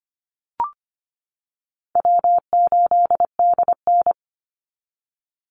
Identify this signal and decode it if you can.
Morse code, heard as a keyed tone, about 25 wpm, 700 Hz. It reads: W8DN
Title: Call Signs - 25wpm